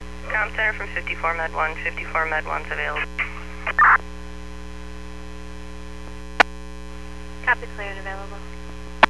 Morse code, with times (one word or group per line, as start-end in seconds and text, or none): none